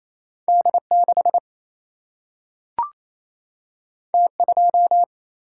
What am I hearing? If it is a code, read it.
Morse code, heard as a keyed tone, about 28 wpm, 700 Hz. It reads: D 6 T 2
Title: Single-Letter Single-Number 28wpm